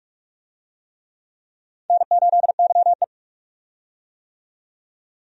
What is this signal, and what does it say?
Morse code, heard as a keyed tone, about 46 wpm, 700 Hz. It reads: N8YE
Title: USA Call Signs - 4 Characters - 45wpm